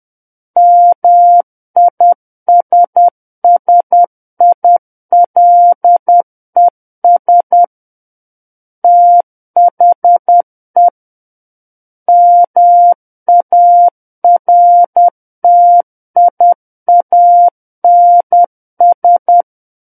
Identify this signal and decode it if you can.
Morse code, heard as a keyed tone, about 10 wpm, 700 Hz. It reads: MISSILES THE MARTIANS
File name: Book One / 01-Chapter 1 - 10 WPM